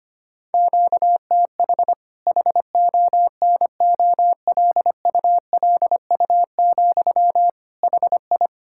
Morse code, heard as a keyed tone, about 25 wpm, 700 Hz. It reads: QTH HONOLULU, HI
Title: K1USN SST QSO Practice 25wpm